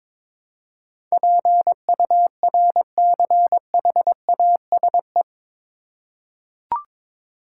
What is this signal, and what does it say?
Morse code, heard as a keyed tone, about 22 wpm, 700 Hz. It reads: PURCHASE
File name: practice / 22wpm / Eight Letter Words - 22wpm